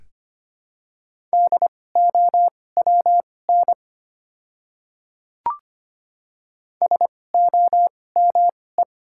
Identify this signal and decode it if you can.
Morse code, heard as a keyed tone, about 25 wpm, spaced as for 18 wpm, 700 Hz. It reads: DOWN SOME
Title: 2x Character Spacing - Top 100 Words - 25wpm